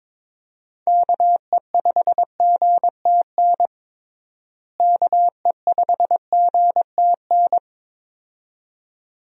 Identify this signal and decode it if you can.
Morse code, heard as a keyed tone, about 22 wpm, 700 Hz. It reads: KE5GTN KE5GTN
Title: USA Call Signs - 6 Characters - Repeated 2x - 22wpm